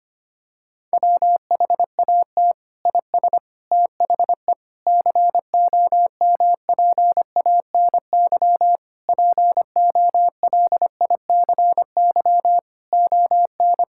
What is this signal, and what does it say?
Morse code, heard as a keyed tone, about 25 wpm, 700 Hz. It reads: WHAT IS THE COMPANY POLICY ON
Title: Sentences from Top 1000 Words - Review Part 5 - 25wpm